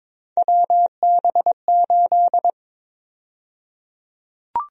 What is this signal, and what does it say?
Morse code, heard as a keyed tone, about 22 wpm, 700 Hz. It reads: WB8
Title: USA Call Sign Prefixes - Repeated 2x - 22wpm